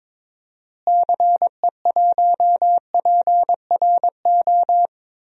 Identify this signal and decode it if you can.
Morse code, heard as a keyed tone, about 22 wpm, 700 Hz. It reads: CE1PRO
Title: DXCC-100 Call Signs - 6 Characters - Repeated 2x - 22wpm